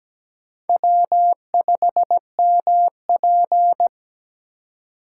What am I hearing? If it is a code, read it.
Morse code, heard as a keyed tone, about 17 wpm, 700 Hz. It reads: W5MP
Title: USA Call Signs - 4 Characters - 3x Character Spacing - 17wpm